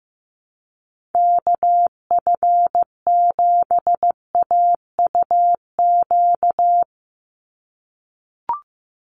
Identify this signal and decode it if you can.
Morse code, heard as a keyed tone, about 15 wpm, 700 Hz. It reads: KF7AUQ
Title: USA Call Signs 15wpm